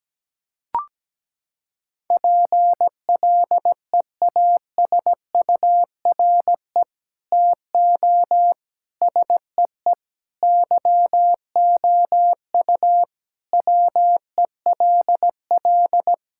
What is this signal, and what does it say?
Morse code, heard as a keyed tone, about 17 wpm, 700 Hz. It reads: PLEASURE TO SEE YOU WELL
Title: Sentences from Top 1600 Words 17wpm